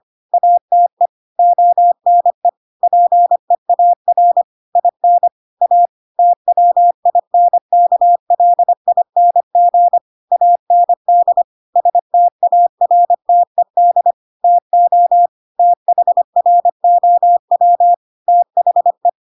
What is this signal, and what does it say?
Morse code, heard as a keyed tone, about 25 wpm, 700 Hz. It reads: ATE ONE PEAR IN A TWINKLING AND STARTED TO THROW THE
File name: Pinocchio / Pinocchio ch7 25 25 ik1hge